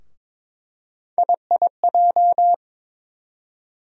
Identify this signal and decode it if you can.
Morse code, heard as a keyed tone, about 22 wpm, 700 Hz. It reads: IIJ
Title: USA Call Sign Postfixes - 3 Letters 22wpm